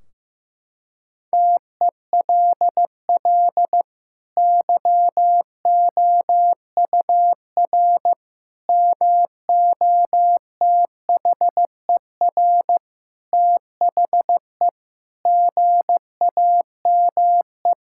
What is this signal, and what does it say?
Morse code, heard as a keyed tone, about 15 wpm, 700 Hz. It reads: TELL YOUR MOTHER THE GAME
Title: Sentences from Top 1600 Words 15wpm